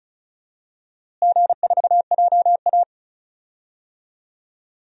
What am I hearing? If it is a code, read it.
Morse code, heard as a keyed tone, about 36 wpm, 700 Hz. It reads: G4JA
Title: DXCC-100 Call Signs - 4 Characters - Repeated 2x - 35wpm